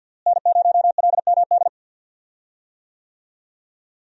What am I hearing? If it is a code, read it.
Morse code, heard as a keyed tone, about 52 wpm, 700 Hz. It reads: N0LMD
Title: USA Call Signs - 5 Characters - Repeated 2x - 50wpm